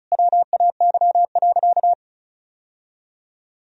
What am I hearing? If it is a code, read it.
Morse code, heard as a keyed tone, about 34 wpm, 700 Hz. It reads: WAY.
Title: Sentences from Top 1000 Words - Review Part 1 - 35wpm